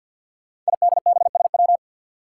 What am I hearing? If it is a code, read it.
Morse code, heard as a keyed tone, about 52 wpm, 700 Hz. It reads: IDBSW
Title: Five Challenging Letter-Number Combinations 50wpm